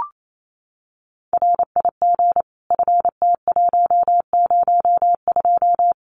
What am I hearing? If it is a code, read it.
Morse code, heard as a keyed tone, about 28 wpm, 700 Hz. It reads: RIG FT102
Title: Common QSO Phrases 28wpm